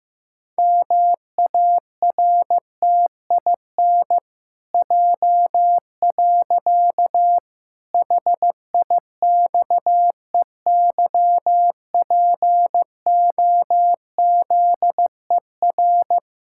Morse code, heard as a keyed tone, about 16 wpm, 700 Hz. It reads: MARTIN J. HIXEYPOZER
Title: Pangram Challenge v2 - 15wpm